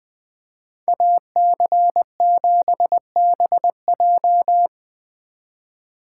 Morse code, heard as a keyed tone, about 20 wpm, 700 Hz. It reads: AC7BJ
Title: Speed Racing - USA Call Signs 17wpm